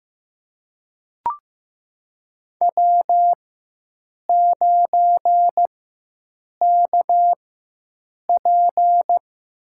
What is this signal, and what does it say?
Morse code, heard as a keyed tone, about 15 wpm, spaced as for 7 wpm, 700 Hz. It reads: W9KP
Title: USA Call Signs - 4 Characters - 4x Character Spacing - 15wpm